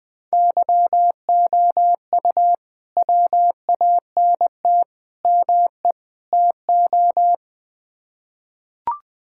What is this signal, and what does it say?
Morse code, heard as a keyed tone, about 20 wpm, 700 Hz. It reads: YOU WANT ME TO
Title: Build A Phrase - 4-Word Phrases - 20wpm